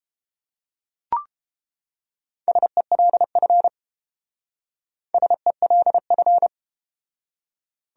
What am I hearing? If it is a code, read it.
Morse code, heard as a keyed tone, about 31 wpm, 700 Hz. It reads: SELF SELF